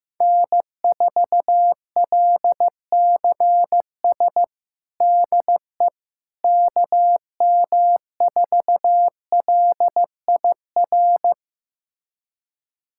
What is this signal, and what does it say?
Morse code, heard as a keyed tone, about 16 wpm, 700 Hz. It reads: N4LCS DE KM4LIR
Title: USA Callsign DE Callsign 15wpm